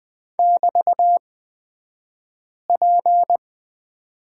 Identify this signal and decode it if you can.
Morse code, heard as a keyed tone, about 20 wpm, spaced as for 5 wpm, 700 Hz. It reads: <BT>P
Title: 223 Lesson 36d - BT - Two Characters 20wpm